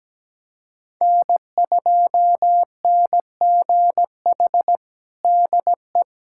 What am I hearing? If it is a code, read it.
Morse code, heard as a keyed tone, about 17 wpm, 700 Hz. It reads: N2NGH DE
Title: USA Callsign DE Callsign 17wpm